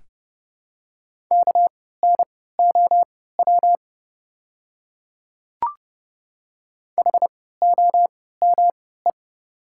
Morse code, heard as a keyed tone, about 30 wpm, spaced as for 17 wpm, 700 Hz. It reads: KNOW HOME
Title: 3x Character Spacing - Top 200 Words - 30wpm